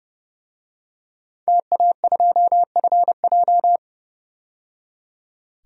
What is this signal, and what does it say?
Morse code, heard as a keyed tone, about 31 wpm, 700 Hz. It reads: TA2FJ
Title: DXCC-100 Call Signs - 5 Characters - 30wpm